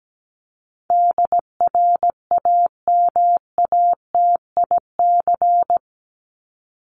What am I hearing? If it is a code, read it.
Morse code, heard as a keyed tone, about 17 wpm, 700 Hz. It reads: DRAMATIC
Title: Top 1251-1500 Words 17wpm